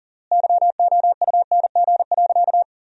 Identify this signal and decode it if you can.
Morse code, heard as a keyed tone, about 40 wpm, 700 Hz. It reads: YOUNG.
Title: 3x Word-Spacing - Sentences from Top 400 Words - 40wpm